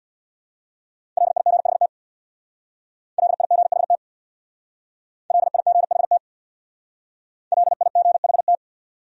Morse code, heard as a keyed tone, about 62 wpm, 700 Hz. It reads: LIGHT LIGHT LIGHT LIGHT
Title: Speed-Racing Top 500 Words 50wpm